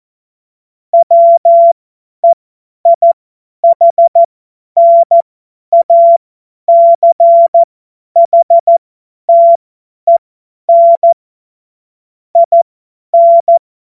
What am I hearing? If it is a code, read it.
Morse code, heard as a keyed tone, about 14 wpm, spaced as for 10 wpm, 680 Hz. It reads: WEIHNACHTEN IN
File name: Morsekurs / Witze / Witz 18. April 2023 50 F70